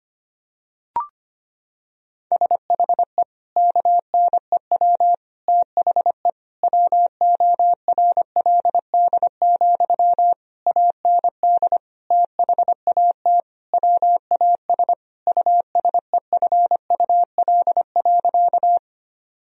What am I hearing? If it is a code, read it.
Morse code, heard as a keyed tone, about 25 wpm, 690 Hz. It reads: SHE KNEW THE WORLD, AND THAT WAS USEFUL.
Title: Sentences from Top 1000 Words - Review Part 4 - 25wpm